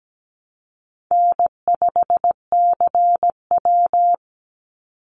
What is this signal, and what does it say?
Morse code, heard as a keyed tone, about 17 wpm, 700 Hz. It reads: N5CW
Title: Easy Call Signs Repeated 2x - 17wpm